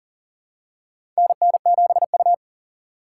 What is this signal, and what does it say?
Morse code, heard as a keyed tone, about 40 wpm, 700 Hz. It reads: NN7U
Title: USA Call Signs - 4 Characters - 3x Character Spacing - 40wpm